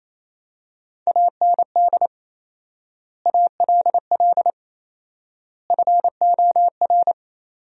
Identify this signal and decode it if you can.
Morse code, heard as a keyed tone, about 28 wpm, 700 Hz. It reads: AND ALL FOR
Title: 4x Word-Spacing - Sentences from Top 100 Words - 28wpm